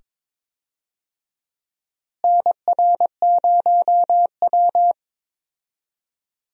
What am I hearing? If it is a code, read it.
Morse code, heard as a keyed tone, about 22 wpm, 700 Hz. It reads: NR0W
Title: USA Call Signs - 4 Characters - Repeated 2x - 22wpm